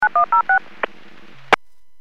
DTMF, heard as telephone touch tones, tones of about 110 ms. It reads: 9106
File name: WPS / 00952 15-15 WPS 451.25 WPS